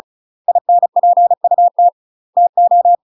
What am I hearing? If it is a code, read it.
Morse code, heard as a keyed tone, about 36 wpm, 700 Hz. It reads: INPUT TO
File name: UK MASTER QSOs TRAINING / 35WPM / UK MASTER QSOs 0011